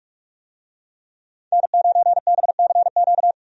Morse code, heard as a keyed tone, about 44 wpm, 700 Hz. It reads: N9BCQ
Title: USA Call Signs - 5 Characters - 45wpm